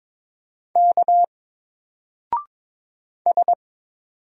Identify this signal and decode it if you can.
Morse code, heard as a keyed tone, about 22 wpm, 700 Hz. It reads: K S
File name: Single-Letter-Number-22wpm